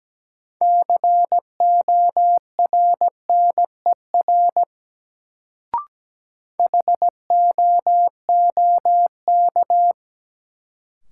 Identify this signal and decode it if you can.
Morse code, heard as a keyed tone, about 17 wpm, 700 Hz. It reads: CORNER HOOK
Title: Sentences from Top 1600 Words - New Word Drill Alt - 17wpm